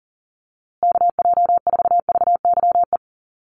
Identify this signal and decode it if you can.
Morse code, heard as a keyed tone, about 40 wpm, 700 Hz. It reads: KJ4VYE
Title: USA Call Signs Repeated 2x 40wpm